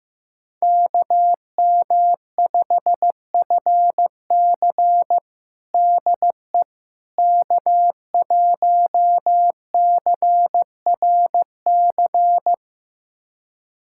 Morse code, heard as a keyed tone, about 16 wpm, 700 Hz. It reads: KM5FC DE K1CRC
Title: USA Callsign DE Callsign - 5 Characters - 15wpm